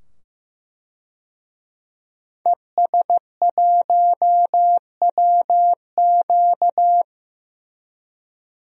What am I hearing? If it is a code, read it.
Morse code, heard as a keyed tone, about 15 wpm, 700 Hz. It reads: ES1WQ
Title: DXCC-100 Call Signs - 5 Characters - 15wpm